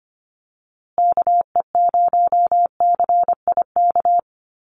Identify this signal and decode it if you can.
Morse code, heard as a keyed tone, about 25 wpm, 700 Hz. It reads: KE0CIK
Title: USA Call Signs Repeated 3x 25wpm